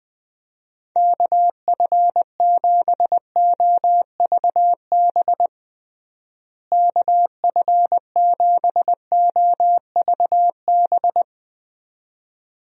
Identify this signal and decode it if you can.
Morse code, heard as a keyed tone, about 20 wpm, 700 Hz. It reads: KF7OVB KF7OVB
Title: USA Call Signs - 6 Characters - Repeated 2x - 20wpm